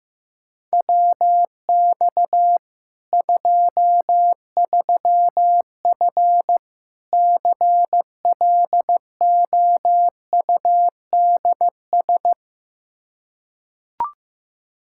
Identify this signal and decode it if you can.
Morse code, heard as a keyed tone, about 15 wpm, 700 Hz. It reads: WX 23F CLOUDS
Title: Practice QSO - Exchange 2 - Info - 15wpm